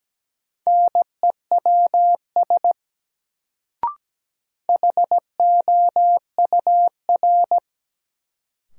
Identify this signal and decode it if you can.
Morse code, heard as a keyed tone, about 17 wpm, 700 Hz. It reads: NEWS HOUR
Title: Sets of 5 Words - New Word Drill - 17wpm